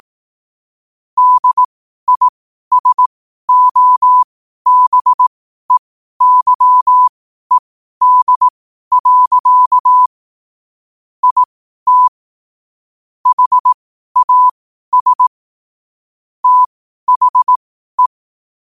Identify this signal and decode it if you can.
Morse code, heard as a keyed tone, about 18 wpm, spaced as for 12 wpm, 1000 Hz. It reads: DISOBEYED. IT HAS THE
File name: qotdfiles / 12wpm